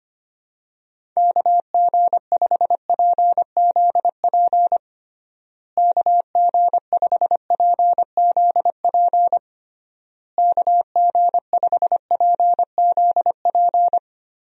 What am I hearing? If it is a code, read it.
Morse code, heard as a keyed tone, about 25 wpm, 700 Hz. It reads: KG5PZP KG5PZP KG5PZP
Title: USA Call Signs - 6 Characters - Repeated 3x - 25wpm